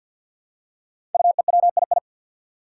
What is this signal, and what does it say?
Morse code, heard as a keyed tone, about 48 wpm, 700 Hz. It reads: UEWII
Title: Five Challenging Letter-Number Combinations 50wpm